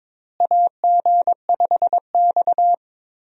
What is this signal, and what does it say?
Morse code, heard as a keyed tone, about 22 wpm, 700 Hz. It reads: AG5X
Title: USA Call Signs - 4 Characters - Repeated 3x - 22wpm